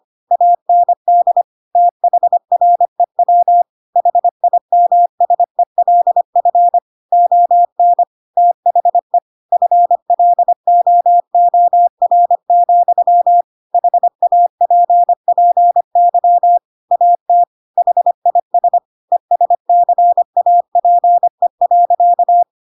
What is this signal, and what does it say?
Morse code, heard as a keyed tone, about 25 wpm, 700 Hz. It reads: AND THREW HIMSELF ON THE FLOOR, HAPPY AT HIS ESCAPE.
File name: Pinocchio / Pinocchio ch4 25 25 ik1hge